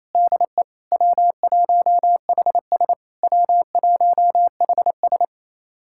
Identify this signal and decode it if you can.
Morse code, heard as a keyed tone, about 28 wpm, 700 Hz. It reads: DE W1HS W1HS
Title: SOTA QSO Practice 28wpm